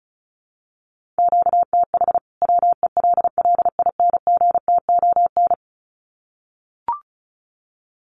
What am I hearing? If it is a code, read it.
Morse code, heard as a keyed tone, about 34 wpm, 700 Hz. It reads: QTH WELLINGTON
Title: Common QSO Phrases 35wpm